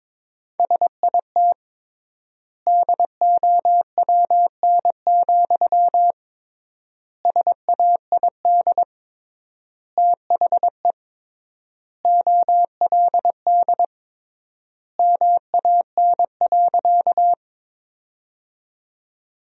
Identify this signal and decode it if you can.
Morse code, heard as a keyed tone, about 22 wpm, 700 Hz. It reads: SIT DOWN, SAID THE OLD MAN.
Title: 3x Word-Spacing - Sentences from Top 500 Words - 22wpm